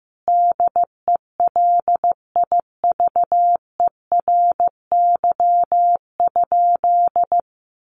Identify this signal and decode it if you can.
Morse code, heard as a keyed tone, about 15 wpm, 700 Hz. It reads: DELIVERY?
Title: Sentences from Top 1500 Words 15wpm